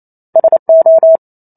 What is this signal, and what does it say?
Morse code, heard as a keyed tone, about 28 wpm, 650 Hz. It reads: SO